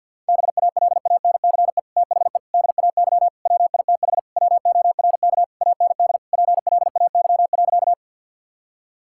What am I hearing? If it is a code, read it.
Morse code, heard as a keyed tone, about 50 wpm, 700 Hz. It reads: BALANCE THE DAY WITH WORK AND PLAY.
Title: Sentences from Top 1000 Words - Review Part 6 - 50wpm